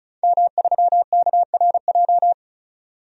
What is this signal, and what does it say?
Morse code, heard as a keyed tone, about 36 wpm, 700 Hz. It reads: M3KRJ
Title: DXCC-100 Call Signs - 5 Characters - 35wpm